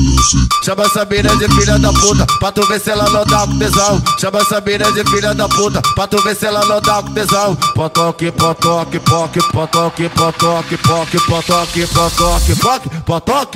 There is no code.